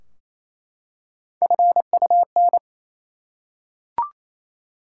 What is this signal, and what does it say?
Morse code, heard as a keyed tone, about 28 wpm, 700 Hz. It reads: FUN
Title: Three Letter Words - 28wpm